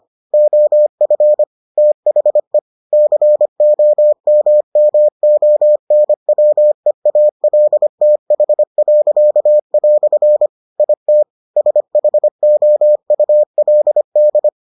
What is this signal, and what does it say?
Morse code, heard as a keyed tone, about 25 wpm, 600 Hz. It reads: OF THE COMMONWEALTH." IT SHOULD